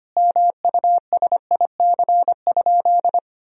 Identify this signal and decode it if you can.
Morse code, heard as a keyed tone, about 25 wpm, 700 Hz. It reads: MUSIC?
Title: Sentences from Top 1000 Words - Review Part 3 - 25wpm